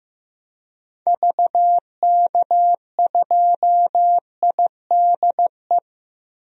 Morse code, heard as a keyed tone, about 15 wpm, 700 Hz. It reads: VK2IDE
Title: DXCC-100 Call Signs - 6 Characters - Repeated 2x - 15wpm